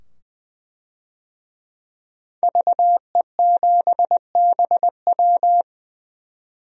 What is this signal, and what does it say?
Morse code, heard as a keyed tone, about 20 wpm, 700 Hz. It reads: VE7BW
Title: DXCC-100 Call Signs - 5 Characters - 20wpm